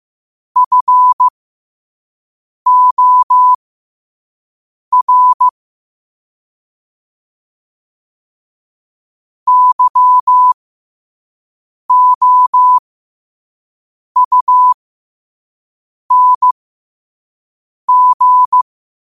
Morse code, heard as a keyed tone, about 15 wpm, spaced as for 5 wpm, 1000 Hz. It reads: FOR YOUNG
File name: qotdfiles / 5wpm